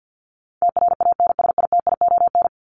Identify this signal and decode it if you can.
Morse code, heard as a keyed tone, about 50 wpm, 700 Hz. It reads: TRANSITION
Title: Top 1751-2000 Words 50wpm